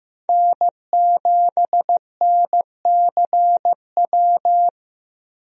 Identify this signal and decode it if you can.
Morse code, heard as a keyed tone, about 15 wpm, 700 Hz. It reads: N7NCW
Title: USA Call Signs - 5 Characters - Repeated 2x - 15wpm